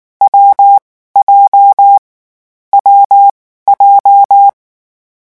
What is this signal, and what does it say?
Morse code, heard as a keyed tone, about 19 wpm, spaced as for 14 wpm, 800 Hz. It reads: WJ WJ
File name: morse mp3 / lecon7-vitesse 18